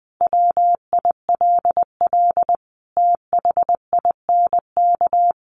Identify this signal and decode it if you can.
Morse code, heard as a keyed tone, about 20 wpm, 700 Hz. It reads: WILL THINK
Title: Sentences from Top 1400 Words 20wpm